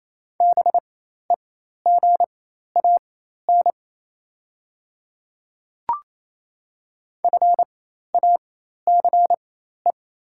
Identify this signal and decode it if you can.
Morse code, heard as a keyed tone, about 27 wpm, spaced as for 13 wpm, 700 Hz. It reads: BEGAN FACE
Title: 4x Character Spacing - Top 300 Words - 28wpm